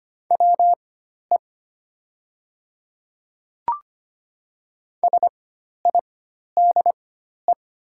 Morse code, heard as a keyed tone, about 25 wpm, spaced as for 12 wpm, 700 Hz. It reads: WE SIDE
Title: 4x Character Spacing - Top 300 Words - 25wpm